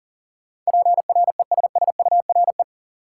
Morse code, heard as a keyed tone, about 38 wpm, 700 Hz. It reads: PRESSURE
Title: Sentences from Top 1000 Words - Review Part 6 - New Word Drill - 40wpm